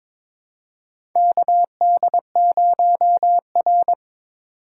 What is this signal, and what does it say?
Morse code, heard as a keyed tone, about 22 wpm, 700 Hz. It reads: KD0R
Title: USA Call Signs - 6x Character Spacing - 22wpm